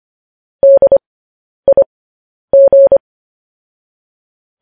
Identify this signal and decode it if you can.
Morse code, heard as a keyed tone, about 25 wpm, spaced as for 10 wpm, 550 Hz. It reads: DIG